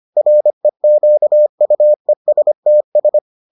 Morse code, heard as a keyed tone, about 25 wpm, 600 Hz. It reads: REQUESTS